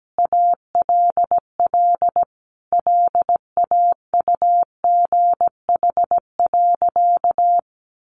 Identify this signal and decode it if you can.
Morse code, heard as a keyed tone, about 17 wpm, 700 Hz. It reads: ALL LAUGH.
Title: Sentences from Top 1500 Words 17wpm